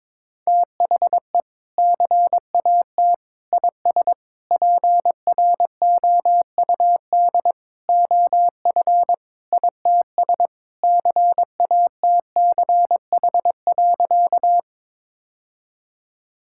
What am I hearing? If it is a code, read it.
Morse code, heard as a keyed tone, about 22 wpm, 700 Hz. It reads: THE CAT IS PROUD OF ITS CATCH.
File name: practice / 22wpm / Sentences with 2-Word Phrases - 22wpm